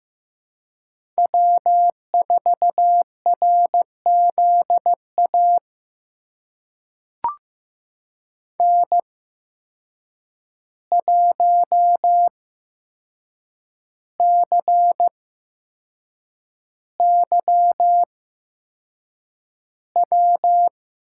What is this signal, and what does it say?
Morse code, heard as a keyed tone, about 15 wpm, spaced as for 6 wpm, 700 Hz. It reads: W4RZA N1CYW
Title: USA Call Signs - 5 Characters - 8x Character Spacing - 15wpm